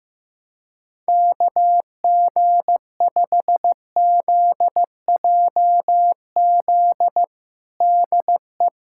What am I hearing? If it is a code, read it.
Morse code, heard as a keyed tone, about 15 wpm, 700 Hz. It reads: KG5ZJZ DE
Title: USA Callsign DE Callsign - 6 Characters - 15wpm